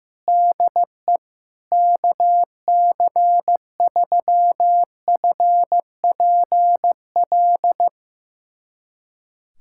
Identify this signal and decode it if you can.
Morse code, heard as a keyed tone, about 15 wpm, 700 Hz. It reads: DE KC3FPL
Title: USA Callsign DE Callsign - 6 Characters - 15wpm